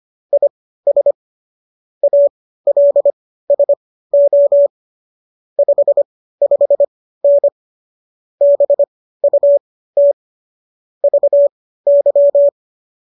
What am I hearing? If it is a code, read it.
Morse code, heard as a keyed tone, about 25 wpm, spaced as for 15 wpm, 570 Hz. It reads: IS ALSO 55N BUT VY